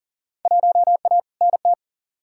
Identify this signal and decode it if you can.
Morse code, heard as a keyed tone, about 39 wpm, 700 Hz. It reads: 1A NT
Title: Field Day Exchanges Encore - 40wpm